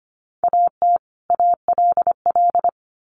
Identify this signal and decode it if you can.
Morse code, heard as a keyed tone, about 25 wpm, 700 Hz. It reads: AT ALL
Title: Sets of 2 Words 25wpm